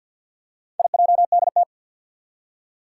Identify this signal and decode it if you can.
Morse code, heard as a keyed tone, about 47 wpm, 700 Hz. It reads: IJDT
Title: Four Challenging Letter-Number Combinations 50wpm